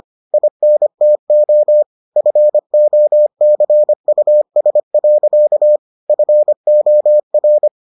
Morse code, heard as a keyed tone, about 25 wpm, 600 Hz. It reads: INTO FOCUS. FOR